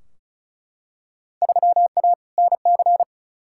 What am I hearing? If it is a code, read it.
Morse code, heard as a keyed tone, about 36 wpm, 700 Hz. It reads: 3A NC
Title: Field Day Exchanges - 35wpm